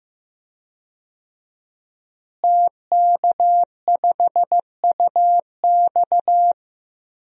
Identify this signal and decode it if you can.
Morse code, heard as a keyed tone, about 15 wpm, 700 Hz. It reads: TK5UX
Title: DXCC-100 Call Signs - 5 Characters - Repeated 3x - 15wpm